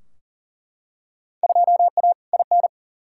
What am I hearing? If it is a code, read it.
Morse code, heard as a keyed tone, about 40 wpm, 700 Hz. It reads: 2A IN
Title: Field Day Exchanges - 40wpm